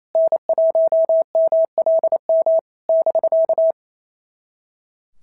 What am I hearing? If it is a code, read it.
Morse code, heard as a keyed tone, about 27 wpm, 650 Hz. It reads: N1MLM <BK>
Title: SOTA QSO Practice 28wpm